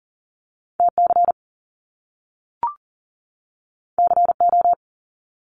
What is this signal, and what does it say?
Morse code, heard as a keyed tone, about 38 wpm, 700 Hz. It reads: TC CO
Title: Top Two Letter Combinations in Words 40wpm